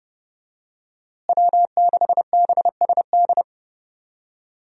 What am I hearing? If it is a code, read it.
Morse code, heard as a keyed tone, about 29 wpm, 700 Hz. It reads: W6BSD
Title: USA Call Signs - 5 Characters - Repeated 3x - 30wpm